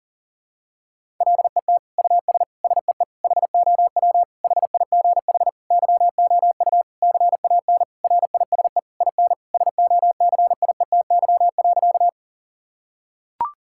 Morse code, heard as a keyed tone, about 40 wpm, 700 Hz. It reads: LET US SEE HOW HIGH YOU CAN RISE IN SOCIETY.
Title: Sentences from Top 1000 Words - Review Part 6 - 40wpm